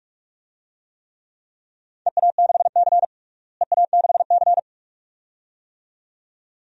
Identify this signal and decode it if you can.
Morse code, heard as a keyed tone, about 42 wpm, 700 Hz. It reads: EA6C EA6C
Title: DXCC-100 Call Signs - 4 Characters - Repeated 2x - 45wpm